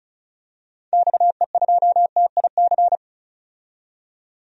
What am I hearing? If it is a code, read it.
Morse code, heard as a keyed tone, about 34 wpm, 700 Hz. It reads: XE2TIC
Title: DXCC-100 Call Signs - 6 Characters - 35wpm